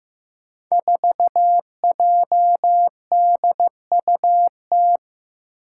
Morse code, heard as a keyed tone, about 15 wpm, 700 Hz. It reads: 4JDUT
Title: Five Challenging Letter-Number Combinations 15wpm